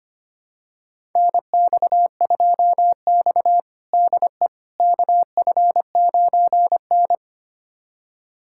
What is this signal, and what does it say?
Morse code, heard as a keyed tone, about 25 wpm, 700 Hz. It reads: NX2X DE KF9N
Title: USA Callsign DE Callsign - 4 Characters - 25wpm